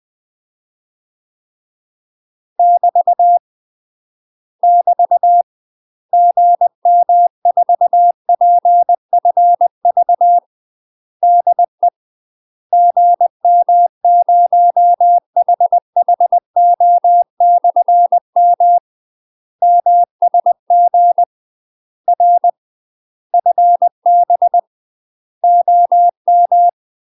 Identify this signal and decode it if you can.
Morse code, heard as a keyed tone, about 20 wpm, 700 Hz. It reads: <BT> <BT> GM4PFV DE GM0HHO/M MSG R FB OM
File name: UK MASTER QSOs TRAINING/20WPM